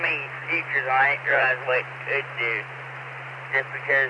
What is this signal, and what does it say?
Emergency Alert System (SAME) tones, heard as harsh buzzing bursts, with speech. It is heard from 1.0 to 3.8 s.